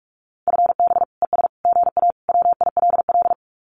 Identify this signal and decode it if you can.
Morse code, heard as a keyed tone, about 44 wpm, 700 Hz. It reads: FB ES GA WILL
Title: Common QSO Phrases Encore 45wpm